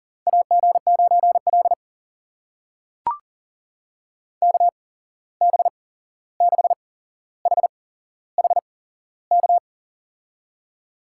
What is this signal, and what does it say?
Morse code, heard as a keyed tone, about 42 wpm, spaced as for 15 wpm, 700 Hz. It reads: AG9L KB6HHK
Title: USA Call Signs - 8x Character Spacing - 40wpm